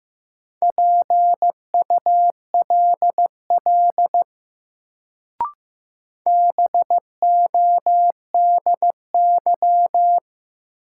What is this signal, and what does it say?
Morse code, heard as a keyed tone, about 15 wpm, 700 Hz. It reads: PULL BODY
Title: Sentences from Top 400 Words - New Word Drill - 15wpm